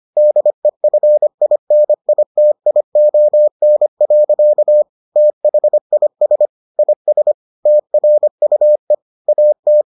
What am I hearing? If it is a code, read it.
Morse code, heard as a keyed tone, about 25 wpm, 600 Hz. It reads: DEFINITION. THIS IS TRUE AT